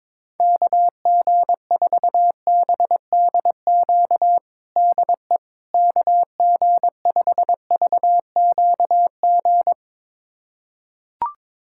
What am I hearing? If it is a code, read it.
Morse code, heard as a keyed tone, about 21 wpm, 700 Hz. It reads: KG4BDQ DE KG5VQG
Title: USA Callsign DE Callsign 22wpm